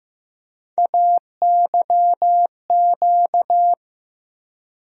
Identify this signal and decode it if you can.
Morse code, heard as a keyed tone, about 15 wpm, 700 Hz. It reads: AYQ
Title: USA Call Sign Postfixes - 3 Letters 15wpm